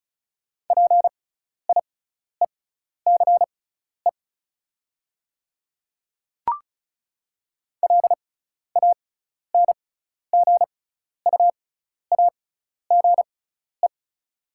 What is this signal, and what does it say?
Morse code, heard as a keyed tone, about 34 wpm, spaced as for 12 wpm, 700 Hz. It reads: PIECE LANGUAGE
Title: 6x Character Spacing - Top 500 Words - 35wpm